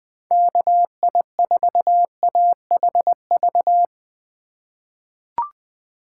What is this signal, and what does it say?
Morse code, heard as a keyed tone, about 21 wpm, 700 Hz. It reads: KI4AHV
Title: USA Call Signs - 6 Characters - 4x Character Spacing - 20wpm